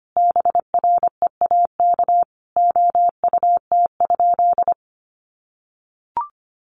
Morse code, heard as a keyed tone, about 25 wpm, 700 Hz. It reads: BREAK OUT?
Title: Sentences from Top 900 Words 25wpm